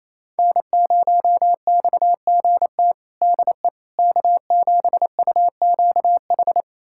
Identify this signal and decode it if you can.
Morse code, heard as a keyed tone, about 28 wpm, 700 Hz. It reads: N0XGT DE K7UQH
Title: USA Callsign DE Callsign - 5 Characters - 28wpm